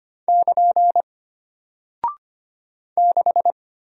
Morse code, heard as a keyed tone, about 25 wpm, 700 Hz. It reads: <KN> 6
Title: Single Character (Letter:Number:Punctuation:Prosign) Alt 25wpm